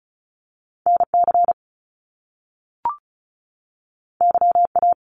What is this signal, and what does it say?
Morse code, heard as a keyed tone, about 36 wpm, 700 Hz. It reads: NC YA
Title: Top Two Letter Combinations in Words 35wpm